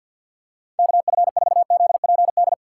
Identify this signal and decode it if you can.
Morse code, heard as a keyed tone, about 50 wpm, 700 Hz. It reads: KF4ZPD
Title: USA Call Signs - 4x Character Spacing - 50wpm